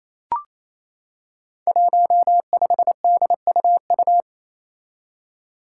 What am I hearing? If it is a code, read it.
Morse code, heard as a keyed tone, about 28 wpm, 700 Hz. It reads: 15DUU
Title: Five Challenging Letter-Number Combinations 28wpm